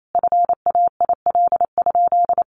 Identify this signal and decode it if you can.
Morse code, heard as a keyed tone, about 28 wpm, 700 Hz. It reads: FAIL?
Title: Sentences from Top 1100 Words 28wpm